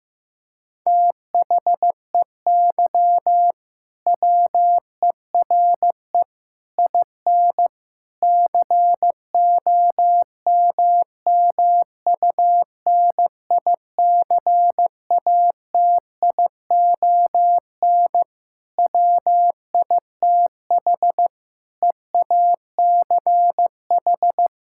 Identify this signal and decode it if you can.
Morse code, heard as a keyed tone, about 15 wpm, 700 Hz. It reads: THEY WERE IN COMMUNICATION WITH EACH
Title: Sentences from Top 1000 Words - Review Part 6 - 15wpm